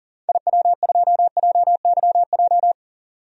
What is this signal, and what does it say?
Morse code, heard as a keyed tone, about 39 wpm, 700 Hz. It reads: IW2JYJ
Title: DXCC-100 Call Signs - 6 Characters - 40wpm